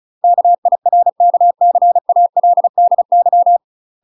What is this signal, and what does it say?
Morse code, heard as a keyed tone, about 33 wpm, 700 Hz. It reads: KIRKCALDY
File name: UK MASTER QSOs TRAINING / 35WPM / UK MASTER QSOs 0007